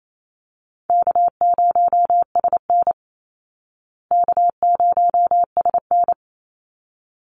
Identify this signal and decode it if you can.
Morse code, heard as a keyed tone, about 28 wpm, 700 Hz. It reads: K0SN K0SN
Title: Easy Call Signs Repeated 2x - 28wpm